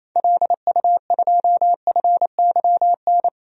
Morse code, heard as a keyed tone, about 28 wpm, 700 Hz. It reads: LU2FYN
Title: DXCC-100 Call Signs - 6 Characters - 28wpm